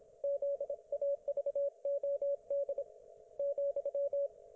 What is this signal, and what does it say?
Morse code, heard as a keyed tone, about 26 wpm, 560 Hz. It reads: ZAVOD ,
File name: Archiv / CUC194 / ok1fhi cul194